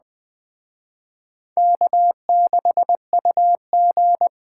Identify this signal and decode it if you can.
Morse code, heard as a keyed tone, about 20 wpm, 700 Hz. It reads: K6UG
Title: Speed Racing - USA 1x2 Call Signs 17wpm